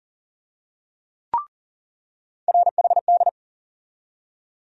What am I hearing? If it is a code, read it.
Morse code, heard as a keyed tone, about 40 wpm, 700 Hz. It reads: RHD
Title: USA Call Sign Postfixes - 3 Letters 40wpm